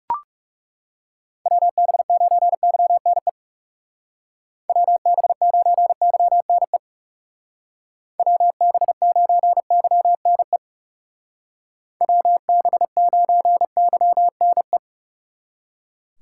Morse code, heard as a keyed tone, about 35 wpm, 700 Hz. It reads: WB9YNE WB9YNE WB9YNE WB9YNE
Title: Speed Racing - USA 2x3 Call Signs 30wpm